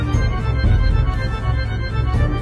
There is no code.